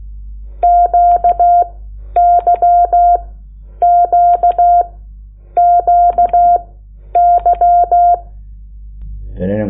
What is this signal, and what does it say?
Morse code, heard as a keyed tone, about 16 wpm, spaced as for 9 wpm, 700 Hz. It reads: QYQQY
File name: Lecciones / CLASE Nº8